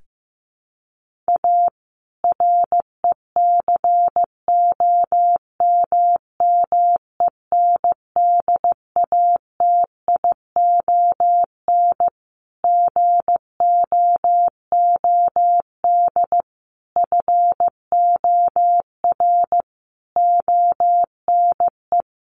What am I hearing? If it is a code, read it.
Morse code, heard as a keyed tone, about 15 wpm, 700 Hz. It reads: A RECOMMENDATION GOOD FOR ONE
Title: Sentences from Top 1400 Words 15wpm